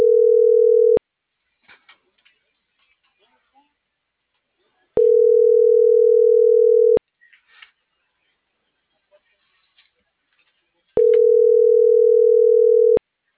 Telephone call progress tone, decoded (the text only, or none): ringback tone